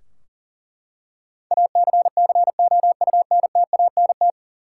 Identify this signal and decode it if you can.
Morse code, heard as a keyed tone, about 40 wpm, 700 Hz. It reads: ACCOUNTANT
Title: Practice QSO - Exchange 3 - More Info - 40wpm